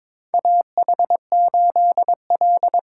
Morse code, heard as a keyed tone, about 22 wpm, 700 Hz. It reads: AH8L
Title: USA Call Signs - 4 Characters - 2x Character Spacing - 22wpm